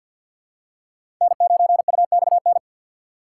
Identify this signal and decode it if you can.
Morse code, heard as a keyed tone, about 48 wpm, 700 Hz. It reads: N9UXN
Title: USA Call Signs - 5 Characters - 3x Character Spacing - 50wpm